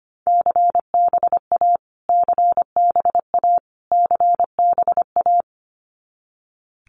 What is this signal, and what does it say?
Morse code, heard as a keyed tone, about 25 wpm, 700 Hz. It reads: CBA CBA CBA
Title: Abbreviations Repeated 3x - 25wpm